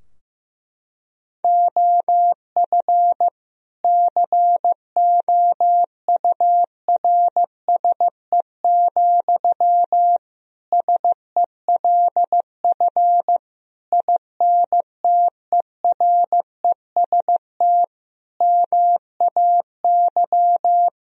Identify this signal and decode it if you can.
Morse code, heard as a keyed tone, about 15 wpm, 700 Hz. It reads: OF COURSE, SELF INTEREST MAY
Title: Sentences from Top 1000 Words - Review Part 2 - 15wpm